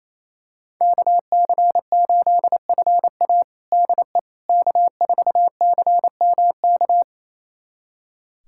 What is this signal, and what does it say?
Morse code, heard as a keyed tone, about 28 wpm, 700 Hz. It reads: KC8FA DE K4CMK
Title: USA Callsign DE Callsign - 5 Characters - 28wpm